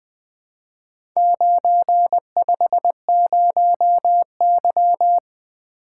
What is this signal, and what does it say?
Morse code, heard as a keyed tone, about 20 wpm, 700 Hz. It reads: 950Y
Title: Four Letter-Number Combinations - 20wpm